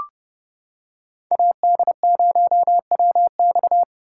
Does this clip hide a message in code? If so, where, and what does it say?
Yes: Morse code from 1.3 to 3.8 s, heard as a keyed tone, about 30 wpm, 700 Hz. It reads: AD0WX